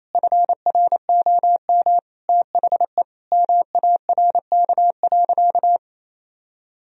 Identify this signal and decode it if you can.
Morse code, heard as a keyed tone, about 28 wpm, 700 Hz. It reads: FROM THE MARK.
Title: Sentences from Top 1000 Words - Review Part 1 - 28wpm